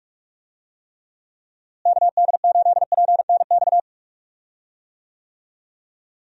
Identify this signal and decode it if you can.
Morse code, heard as a keyed tone, about 46 wpm, 700 Hz. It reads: KD8PNX
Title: USA Call Signs - 6 Characters - Repeated 3x - 45wpm